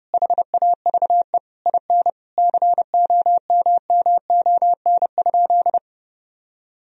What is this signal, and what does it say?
Morse code, heard as a keyed tone, about 32 wpm, 700 Hz. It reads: HAVE IN COMMON?
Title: Sentences from Top 1000 Words - Review Part 3 - 30wpm